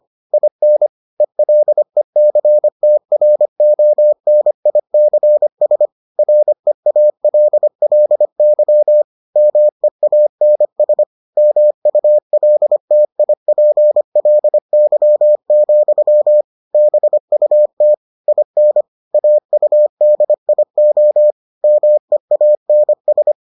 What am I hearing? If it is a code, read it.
Morse code, heard as a keyed tone, about 25 wpm, 600 Hz. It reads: IN ELECTRONICS REALLY MEANS MULTIPLY, BUT IN AUDIO MEANS